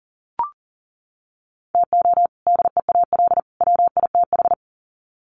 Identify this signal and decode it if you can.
Morse code, heard as a keyed tone, about 40 wpm, 700 Hz. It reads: TO DEAL WITH
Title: Sets of 3 Words 40wpm